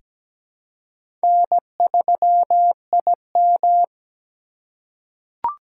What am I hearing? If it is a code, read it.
Morse code, heard as a keyed tone, about 17 wpm, 700 Hz. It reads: N3IM
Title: USA Call Signs - 4 Characters - 2x Character Spacing - 17wpm